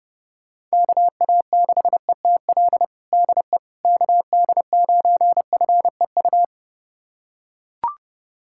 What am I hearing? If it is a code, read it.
Morse code, heard as a keyed tone, about 32 wpm, 690 Hz. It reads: KA6ETL DE KD9FEU
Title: USA Callsign DE Callsign - 6 Characters - 30wpm